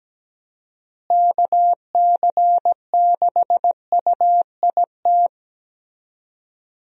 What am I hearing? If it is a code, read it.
Morse code, heard as a keyed tone, about 17 wpm, 700 Hz. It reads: KC6UIT
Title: USA Call Signs - 6 Characters - Repeated 2x - 17wpm